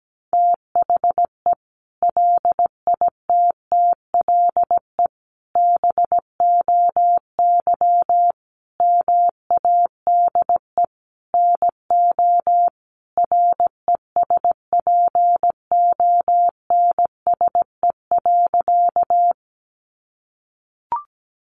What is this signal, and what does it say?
Morse code, heard as a keyed tone, about 17 wpm, 700 Hz. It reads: THE LITTLE BOY MADE NO RESPONSE.
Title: Sentences from Top 1400 Words 17wpm